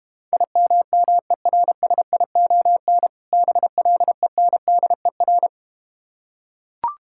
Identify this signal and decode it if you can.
Morse code, heard as a keyed tone, about 32 wpm, 700 Hz. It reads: IMMERSION BLENDER